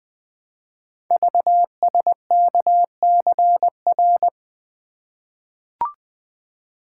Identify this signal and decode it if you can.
Morse code, heard as a keyed tone, about 20 wpm, 700 Hz. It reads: VSKCR
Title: Code Groups - 5 Letters - 20wpm